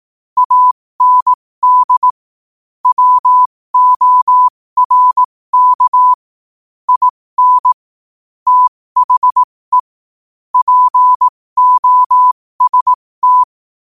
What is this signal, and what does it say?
Morse code, heard as a keyed tone, about 18 wpm, spaced as for 15 wpm, 1000 Hz. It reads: AND WORK IN THE POST
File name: qotdfiles / 15wpm